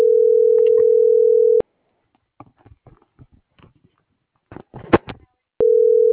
Ringback tone (North America), 440 + 480 Hz.